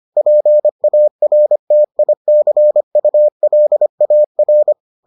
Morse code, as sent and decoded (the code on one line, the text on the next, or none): .--. .- .-. - .. -.-. ..- .-.. .- .-.
PARTICULAR